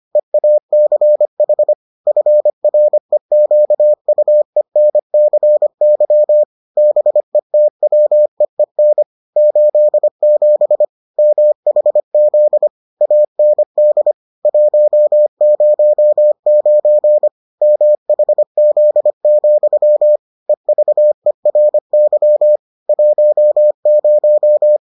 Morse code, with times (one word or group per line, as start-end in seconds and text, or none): 0.2-1.7 EACH
2.1-6.4 FREQUENCY
6.8-9.0 BETWEEN
9.4-10.8 87
11.2-12.7 MHZ
13.0-14.1 AND
14.4-17.3 109
17.6-20.2 MHZ,
20.5-22.6 EVERY
22.9-24.8 10